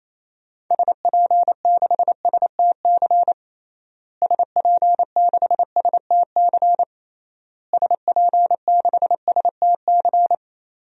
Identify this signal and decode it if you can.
Morse code, heard as a keyed tone, about 28 wpm, 700 Hz. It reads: SP6STC SP6STC SP6STC
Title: DXCC-100 Call Signs - 6 Characters - Repeated 3x - 28wpm